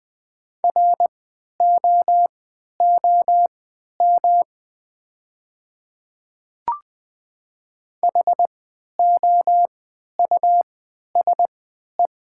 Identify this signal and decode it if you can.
Morse code, heard as a keyed tone, about 20 wpm, spaced as for 11 wpm, 700 Hz. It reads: ROOM HOUSE
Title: 3x Character Spacing - Top 300 Words - 20wpm